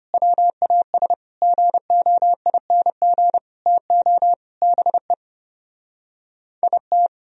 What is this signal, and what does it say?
Morse code, heard as a keyed tone, about 29 wpm, 700 Hz. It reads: WAS GOING TO BE IT
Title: Speed-Racing - Sets of 5 Words 25wpm